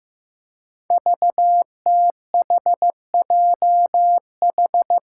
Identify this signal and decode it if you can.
Morse code, heard as a keyed tone, about 15 wpm, 700 Hz. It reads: VTHJH
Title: Five Challenging Letter-Number Combinations 15wpm